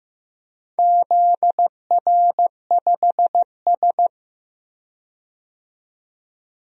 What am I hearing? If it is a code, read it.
Morse code, heard as a keyed tone, about 15 wpm, 700 Hz. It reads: ZR5S
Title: DXCC-100 Call Signs - 4 Characters - 15wpm